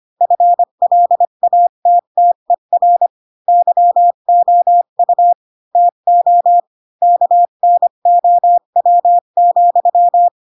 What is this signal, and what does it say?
Morse code, heard as a keyed tone, about 25 wpm, 700 Hz. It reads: FLATTER YOU TO KNOW,